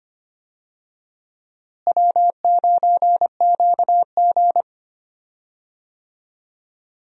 Morse code, heard as a keyed tone, about 25 wpm, 700 Hz. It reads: W9QG
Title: USA Call Signs - 4 Characters - Repeated 2x - 25wpm